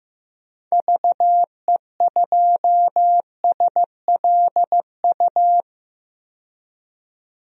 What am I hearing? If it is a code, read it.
Morse code, heard as a keyed tone, about 15 wpm, 700 Hz. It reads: VE2SLU
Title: DXCC-100 Call Signs - 6 Characters - 15wpm